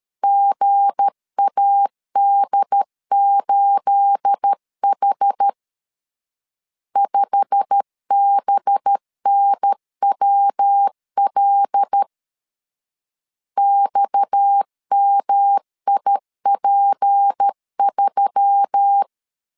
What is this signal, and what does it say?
Morse code, heard as a keyed tone, about 13 wpm, 800 Hz. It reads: GAD8H 5BNWL XMIP3
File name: CW / DET / 1 9 1500C S C 12wpm 3811